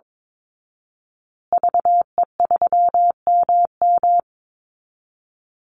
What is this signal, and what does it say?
Morse code, heard as a keyed tone, about 22 wpm, 700 Hz. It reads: VE3MM
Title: Call Signs Repeated 2x - 22wpm